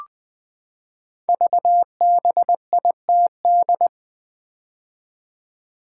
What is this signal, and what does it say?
Morse code, heard as a keyed tone, about 20 wpm, 700 Hz. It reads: VBITD
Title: Code Groups - 5 Letters - 20wpm